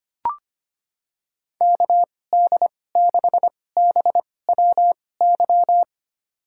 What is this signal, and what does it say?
Morse code, heard as a keyed tone, about 25 wpm, spaced as for 18 wpm, 700 Hz. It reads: KD6BWY
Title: USA Call Signs - 6 Characters - 2x Character Spacing - 25wpm